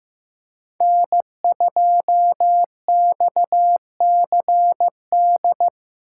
Morse code, heard as a keyed tone, about 15 wpm, 700 Hz. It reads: N2XCD
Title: USA Call Signs - 6x Character Spacing - 15wpm